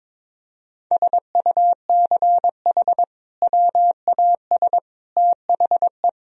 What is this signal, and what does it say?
Morse code, heard as a keyed tone, about 22 wpm, 700 Hz. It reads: SUCH WAS THE
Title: Sentences from Top 1000 Words - Review Part 1 - 22wpm